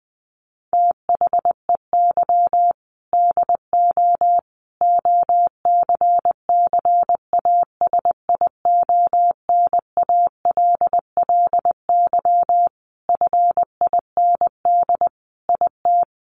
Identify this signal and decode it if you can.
Morse code, heard as a keyed tone, about 19 wpm, 700 Hz. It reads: THEY DO OCCASIONALLY FIND IT
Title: Sentences from Top 1300 Words 20wpm